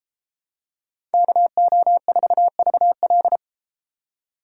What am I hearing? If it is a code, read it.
Morse code, heard as a keyed tone, about 33 wpm, 700 Hz. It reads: KO4VL